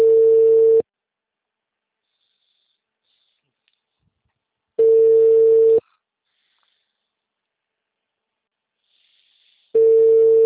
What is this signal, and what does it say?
Ringback tone (China, 450 Hz).
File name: out- -2007- - - .1253